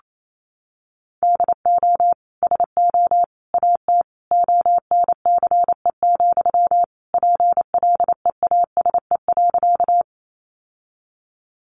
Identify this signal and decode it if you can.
Morse code, heard as a keyed tone, about 28 wpm, 700 Hz. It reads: DO SO AT ONCE, PLEASE.
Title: Sentences from Top 1500 Words 28wpm